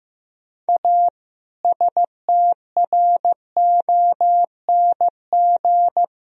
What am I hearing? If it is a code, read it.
Morse code, heard as a keyed tone, about 15 wpm, 700 Hz. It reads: A STRONG
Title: Sentences from Top 1000 Words - Review Part 3 - 15wpm